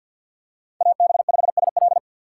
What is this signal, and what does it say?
Morse code, heard as a keyed tone, about 54 wpm, 700 Hz. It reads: AB5SL
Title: USA Call Signs - 5 Characters - 6x Character Spacing - 50wpm